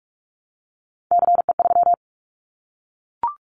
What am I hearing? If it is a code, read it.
Morse code, heard as a keyed tone, about 44 wpm, 700 Hz. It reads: CE3
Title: Callsign Prefixes for Top 100 DXCC - 45wpm